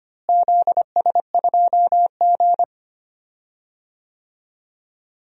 Morse code, heard as a keyed tone, about 25 wpm, 700 Hz. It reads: ZS2G
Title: DXCC-100 Call Signs - 4 Characters - Repeated 3x - 25wpm